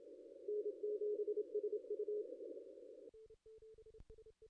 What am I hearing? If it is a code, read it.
Morse code, heard as a keyed tone, about 27 wpm, 440 Hz. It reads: N7SU
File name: contests / ss-cw-k5tr-op-n5rz -0400